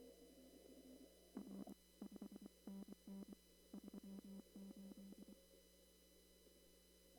Morse code, heard as a keyed tone, about 23 wpm, 200 Hz. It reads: R 5NN 38
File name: CW / kt5j wpx cw-op-k5pi